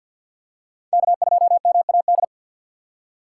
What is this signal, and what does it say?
Morse code, heard as a keyed tone, about 52 wpm, 700 Hz. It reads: K2MAD
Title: USA Call Signs - 5 Characters - 2x Character Spacing - 50wpm